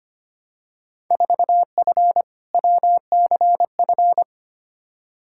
Morse code, heard as a keyed tone, about 25 wpm, 700 Hz. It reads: 4F WCF
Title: Field Day Exchanges - 25wpm